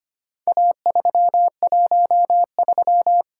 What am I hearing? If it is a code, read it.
Morse code, heard as a keyed tone, about 25 wpm, 700 Hz. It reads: A313
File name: practice / 25wpm / Four Letter-Number Combinations - 25wpm